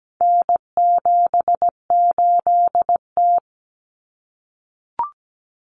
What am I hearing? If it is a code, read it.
Morse code, heard as a keyed tone, about 17 wpm, 700 Hz. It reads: N78T
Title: Common Cut Numbers - 4-Digit 17wpm